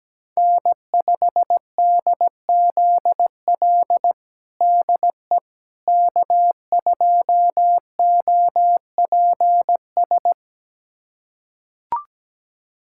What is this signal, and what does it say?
Morse code, heard as a keyed tone, about 17 wpm, 700 Hz. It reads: N5DZL DE K2OPS
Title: USA Callsign DE Callsign - 5 Characters - 17wpm